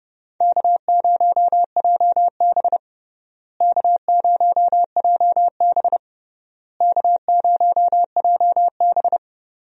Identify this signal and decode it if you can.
Morse code, heard as a keyed tone, about 30 wpm, 700 Hz. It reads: K0JB K0JB K0JB
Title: USA Call Signs - 4 Characters - Repeated 3x - 30wpm